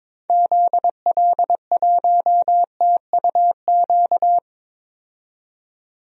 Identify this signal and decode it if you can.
Morse code, heard as a keyed tone, about 22 wpm, 700 Hz. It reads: ZL1TUQ
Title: DXCC-100 Call Signs - 6 Characters - 22wpm